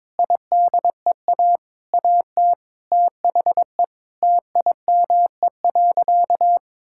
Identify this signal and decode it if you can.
Morse code, heard as a keyed tone, about 22 wpm, 700 Hz. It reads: IDEA AT THE TIME.
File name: practice / 22wpm / Sentences with 8-Word Phrases - 22wpm